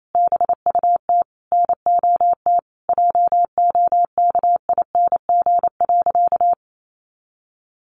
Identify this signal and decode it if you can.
Morse code, heard as a keyed tone, about 28 wpm, 700 Hz. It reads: BUT NOT JOKING.
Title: A Sound Of Thunder 28wpm